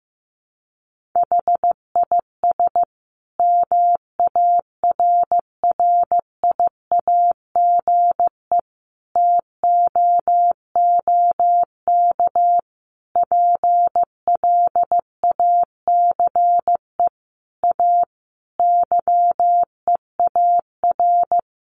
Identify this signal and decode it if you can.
Morse code, heard as a keyed tone, about 15 wpm, 700 Hz. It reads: HIS MARRIAGE TOOK PLACE A YEAR
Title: Sentences from Top 1500 Words 15wpm